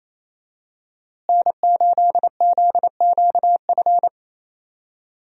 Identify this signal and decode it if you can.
Morse code, heard as a keyed tone, about 28 wpm, 700 Hz. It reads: N8ZQF
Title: USA Call Signs - 5 Characters - Repeated 3x - 28wpm